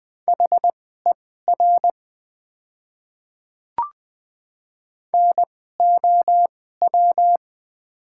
Morse code, heard as a keyed tone, about 20 wpm, spaced as for 14 wpm, 700 Hz. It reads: HER NOW
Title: 2x Character Spacing - Top 300 Words - 20wpm